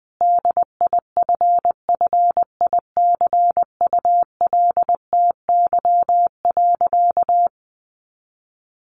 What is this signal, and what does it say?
Morse code, heard as a keyed tone, about 19 wpm, 700 Hz. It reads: DIFFICULTY.
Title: Sentences from Top 1500 Words 20wpm